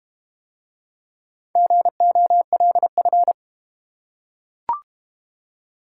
Morse code, heard as a keyed tone, about 30 wpm, 700 Hz. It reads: GOLF
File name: Easy Words - Speed-Racing with Hints - 25wpm